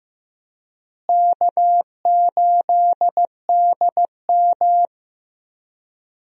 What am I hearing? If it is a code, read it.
Morse code, heard as a keyed tone, about 15 wpm, 700 Hz. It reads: K8DM
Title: USA Call Signs - 4 Characters - Repeated 2x - 15wpm